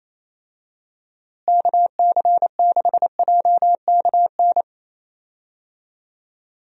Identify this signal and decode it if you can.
Morse code, heard as a keyed tone, about 28 wpm, 700 Hz. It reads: KC6JKN
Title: USA Call Signs - 6 Characters - Repeated 2x - 28wpm